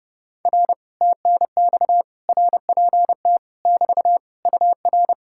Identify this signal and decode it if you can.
Morse code, heard as a keyed tone, about 30 wpm, 700 Hz. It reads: R TNX RPT <BT> UR